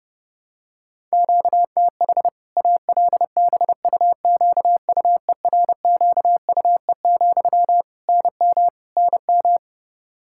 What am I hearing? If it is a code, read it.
Morse code, heard as a keyed tone, about 30 wpm, 700 Hz. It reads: QTH ALBUQUERQUE, NM NM
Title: K1USN SST QSO Practice 30wpm